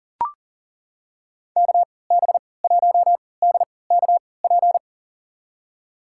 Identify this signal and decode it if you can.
Morse code, heard as a keyed tone, about 40 wpm, spaced as for 23 wpm, 700 Hz. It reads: KB1DKP
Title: USA Call Signs - 6 Characters - 3x Character Spacing - 40wpm